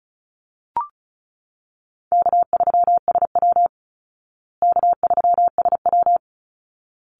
Morse code, heard as a keyed tone, about 34 wpm, 700 Hz. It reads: K3SW K3SW
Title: Easy Call Signs Repeated 2x - 35wpm